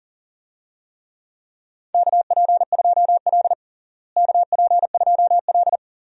Morse code, heard as a keyed tone, about 40 wpm, 700 Hz. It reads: KP2L KP2L
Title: DXCC-100 Call Signs - 4 Characters - Repeated 2x - 40wpm